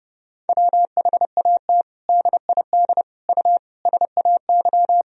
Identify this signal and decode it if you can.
Morse code, heard as a keyed tone, about 30 wpm, 700 Hz. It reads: WHAT DID U SAY
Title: Sentences from Top 200 Words and Abbreviations 30wpm